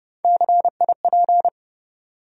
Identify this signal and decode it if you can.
Morse code, heard as a keyed tone, about 30 wpm, 700 Hz. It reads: CIP
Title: USA Call Sign Postfixes - 3 Letters 30wpm